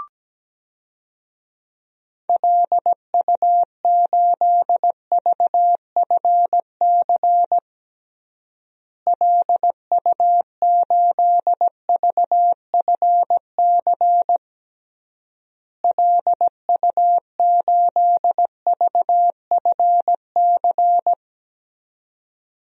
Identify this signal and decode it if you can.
Morse code, heard as a keyed tone, about 17 wpm, 700 Hz. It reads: LU8VFC LU8VFC LU8VFC
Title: DXCC-100 Call Signs - 6 Characters - Repeated 3x - 17wpm